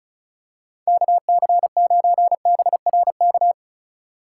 Morse code, heard as a keyed tone, about 36 wpm, 700 Hz. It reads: KC9BRK
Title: USA Call Signs - 6 Characters - 3x Character Spacing - 35wpm